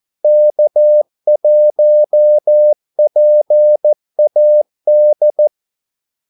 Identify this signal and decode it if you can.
Morse code, heard as a keyed tone, about 14 wpm, 600 Hz. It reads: K1PAD
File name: Callsigns-14wpm